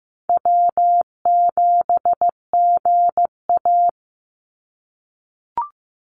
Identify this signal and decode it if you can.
Morse code, heard as a keyed tone, about 15 wpm, 700 Hz. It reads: W7GA
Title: NA SOTA Chaser Callsigns 15wpm